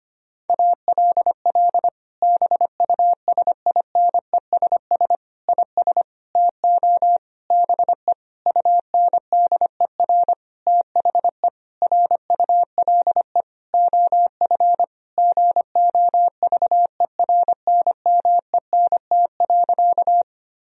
Morse code, heard as a keyed tone, about 25 wpm, 700 Hz. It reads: ALL BUSINESS IS TO BE UNDER THE RULE OF GOVERNMENT.
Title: Sentences from Top 1000 Words - Review Part 3 - 25wpm